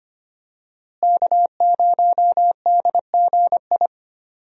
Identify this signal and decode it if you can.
Morse code, heard as a keyed tone, about 25 wpm, 700 Hz. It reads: K0DGI
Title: USA Call Signs - 5 Characters - Repeated 2x - 25wpm